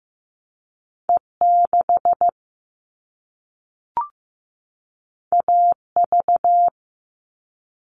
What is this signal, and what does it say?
Morse code, heard as a keyed tone, about 15 wpm, 700 Hz. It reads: E6 AV
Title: All Cut Numbers - 2-Digit 15wpm